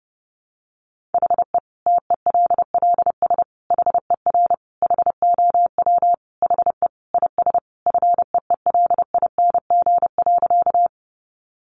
Morse code, heard as a keyed tone, about 30 wpm, 700 Hz. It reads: HE TELLS HER HOW HE IS FEELING.
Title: Additional Easy Phrases 30wpm